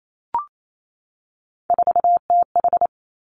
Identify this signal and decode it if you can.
Morse code, heard as a keyed tone, about 28 wpm, 700 Hz. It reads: 4TH
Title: Three Challenging Letter-Number Combinations 28wpm